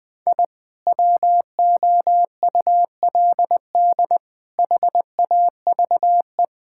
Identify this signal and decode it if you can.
Morse code, heard as a keyed tone, about 19 wpm, 700 Hz. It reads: I WOULD HAVE
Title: Sentences with 2-Word Phrases - Spoken Beginning - 20wpm